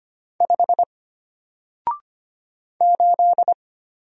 Morse code, heard as a keyed tone, about 25 wpm, 700 Hz. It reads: 5 8
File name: Single-Letter-Number-25wpm